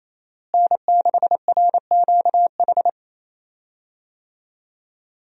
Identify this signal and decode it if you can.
Morse code, heard as a keyed tone, about 27 wpm, 700 Hz. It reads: N6RQH
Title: USA Call Signs - 5 Characters - Repeated 3x - 28wpm